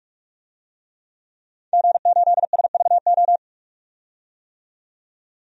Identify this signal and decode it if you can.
Morse code, heard as a keyed tone, about 41 wpm, 700 Hz. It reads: G8SVO
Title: DXCC-100 Call Signs - 5 Characters - 45wpm